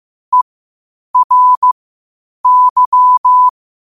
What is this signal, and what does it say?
Morse code, heard as a keyed tone, about 15 wpm, spaced as for 8 wpm, 1000 Hz. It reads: ERY
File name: qotdfiles / 8wpm